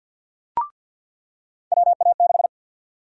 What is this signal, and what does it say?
Morse code, heard as a keyed tone, about 49 wpm, 700 Hz. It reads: WA6
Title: USA Call Sign Prefixes - 50wpm